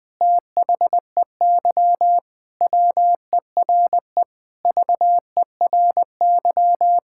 Morse code, heard as a keyed tone, about 20 wpm, 700 Hz. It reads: THEY WERE VERY
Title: Sentences with 6-Word Phrases - 20wpm